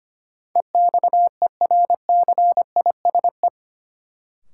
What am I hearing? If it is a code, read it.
Morse code, heard as a keyed tone, about 25 wpm, 700 Hz. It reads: EXERCISE
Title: Sentences from Top 1000 Words - Review Part 4 - New Word Drill - 25wpm